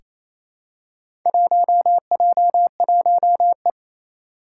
Morse code, heard as a keyed tone, about 27 wpm, 700 Hz. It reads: 1J1E
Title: Four Challenging Letter-Number Combinations 28wpm